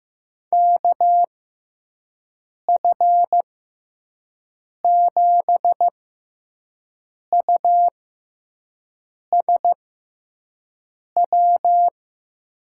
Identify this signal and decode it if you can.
Morse code, heard as a keyed tone, about 15 wpm, spaced as for 5 wpm, 700 Hz. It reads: KF7USW
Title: USA Call Signs - 6 Characters - 6x Character Spacing - 15wpm